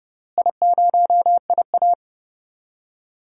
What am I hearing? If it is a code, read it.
Morse code, heard as a keyed tone, about 30 wpm, 700 Hz. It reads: I0IA
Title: DXCC-100 Call Signs - 4 Characters - Repeated 2x - 30wpm